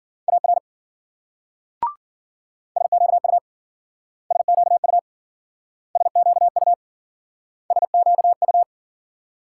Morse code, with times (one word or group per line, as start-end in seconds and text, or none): 0.3-0.6 AR
2.8-3.4 SQU
4.3-5.0 SQU
6.0-6.7 SQU
7.7-8.6 SQU